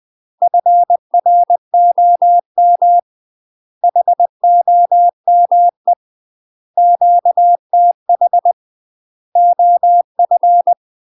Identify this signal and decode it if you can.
Morse code, heard as a keyed tone, about 20 wpm, 700 Hz. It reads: FROM HOME QTH OF
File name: UK MASTER QSOs TRAINING / 20WPM / UK MASTER QSOs 0072